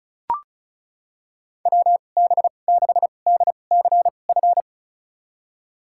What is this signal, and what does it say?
Morse code, heard as a keyed tone, about 36 wpm, spaced as for 25 wpm, 700 Hz. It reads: WB6DCF
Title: USA Call Signs - 6 Characters - 2x Character Spacing - 35wpm